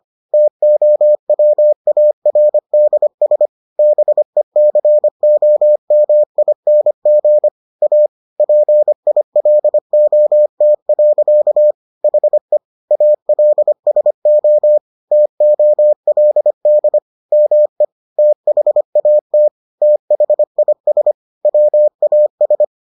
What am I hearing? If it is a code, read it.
Morse code, heard as a keyed tone, about 25 wpm, 600 Hz. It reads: TOWARDS BECOMING A PILOT. HE ALSO TOLD ME THAT THIS WAS